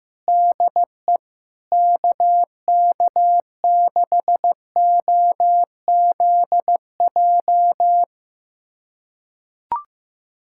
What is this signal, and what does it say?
Morse code, heard as a keyed tone, about 15 wpm, 700 Hz. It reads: DE KK6OZJ
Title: USA Callsign DE Callsign 15wpm